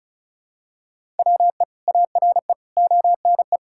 Morse code, heard as a keyed tone, about 34 wpm, 700 Hz. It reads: WE ARE ONE
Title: Sentences from Top 1000 Words - Review Part 6 - 35wpm